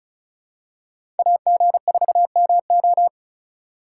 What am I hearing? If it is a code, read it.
Morse code, heard as a keyed tone, about 36 wpm, 700 Hz. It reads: AG4MO
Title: USA Call Signs - 5 Characters - 4x Character Spacing - 35wpm